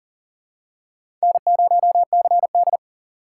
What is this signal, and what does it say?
Morse code, heard as a keyed tone, about 39 wpm, 700 Hz. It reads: N0CD
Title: USA Call Signs - 4 Characters - 2x Character Spacing - 40wpm